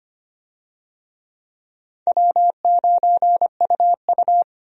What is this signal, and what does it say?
Morse code, heard as a keyed tone, about 25 wpm, 700 Hz. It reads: W9UU
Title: USA Call Signs - 4 Characters - 25wpm